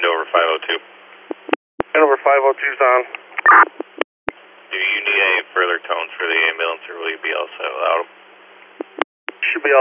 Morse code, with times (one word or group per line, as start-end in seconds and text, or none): none